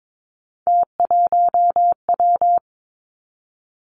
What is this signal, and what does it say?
Morse code, heard as a keyed tone, about 22 wpm, 700 Hz. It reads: T1W
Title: Three Challenging Letter-Number Combinations 22wpm